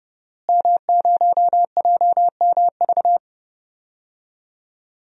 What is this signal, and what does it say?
Morse code, heard as a keyed tone, about 30 wpm, 700 Hz. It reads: M0JMV
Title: DXCC-100 Call Signs - 5 Characters - Repeated 3x - 30wpm